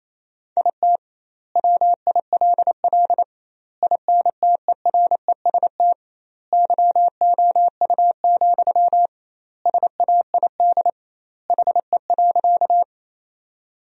Morse code, heard as a keyed tone, about 28 wpm, 700 Hz. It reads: IT WILL INTEREST YOU, SAID HE.
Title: 2x Word-Spacing - Sentences from Top 400 Words - 28wpm